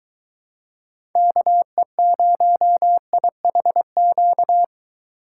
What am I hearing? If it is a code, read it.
Morse code, heard as a keyed tone, about 22 wpm, 700 Hz. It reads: KE0IHQ